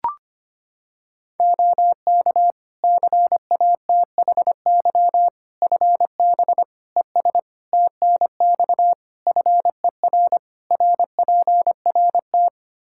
Morse code, heard as a keyed tone, about 25 wpm, 700 Hz. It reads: OK CATHY FB ES TNX FER RPRT
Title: Practice QSO - Exchange 2 - Info - 25wpm